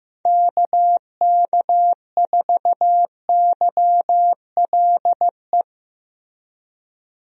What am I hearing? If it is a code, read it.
Morse code, heard as a keyed tone, about 16 wpm, 700 Hz. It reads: KK4YLE
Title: USA Call Signs - 6 Characters - Repeated 2x - 15wpm